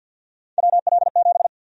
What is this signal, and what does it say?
Morse code, heard as a keyed tone, about 47 wpm, 700 Hz. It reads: WL7
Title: USA Call Sign Prefixes - 50wpm